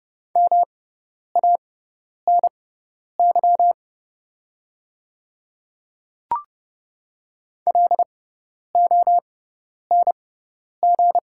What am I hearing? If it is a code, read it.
Morse code, heard as a keyed tone, about 30 wpm, spaced as for 10 wpm, 700 Hz. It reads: MANY LONG
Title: 6x Character Spacing - Top 200 Words - 30wpm